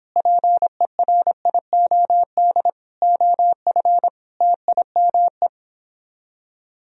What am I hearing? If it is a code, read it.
Morse code, heard as a keyed tone, about 29 wpm, 700 Hz. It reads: PERIOD OF TIME